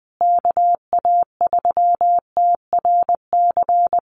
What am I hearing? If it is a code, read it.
Morse code, heard as a keyed tone, about 20 wpm, 700 Hz. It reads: KA3TRC
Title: USA Call Signs Repeated 3x 20wpm